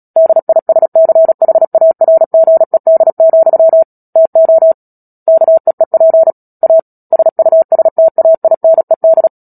Morse code, heard as a keyed tone, about 37 wpm, 650 Hz. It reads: DISCHARGED, TO KEEP A SUSTAINED
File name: Book Two / 10-Chapter 10 - Epilogue - 36 WPM